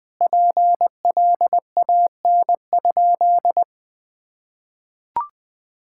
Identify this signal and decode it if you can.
Morse code, heard as a keyed tone, about 21 wpm, 700 Hz. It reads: PLAN?
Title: Sentences with 5-Word Phrases - Spoken Beginning - 20wpm